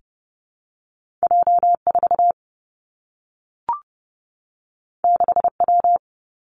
Morse code, heard as a keyed tone, about 31 wpm, 700 Hz. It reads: J4 6W
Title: Two Challenging Letter-Number Combinations 30wpm